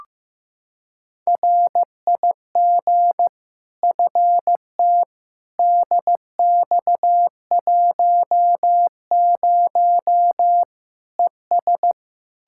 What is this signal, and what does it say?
Morse code, heard as a keyed tone, about 15 wpm, 700 Hz. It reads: RIG FT DX10 ES
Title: Practice QSO - Exchange 2 - Info - 15wpm